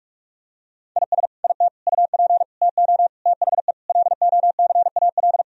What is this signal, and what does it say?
Morse code, heard as a keyed tone, about 46 wpm, 700 Hz. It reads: IS IT UP TO THE LOCAL
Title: Sentences from Top 1000 Words - Review Part 4 - 45wpm